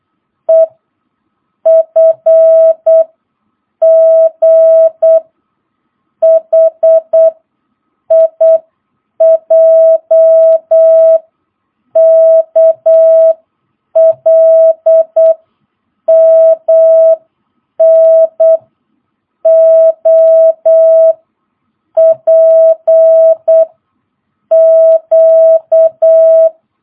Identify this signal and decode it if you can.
Morse code, heard as a keyed tone, about 8 wpm, 650 Hz. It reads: E F G H I J K L M N O P Q
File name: jh8cbh / 2022cw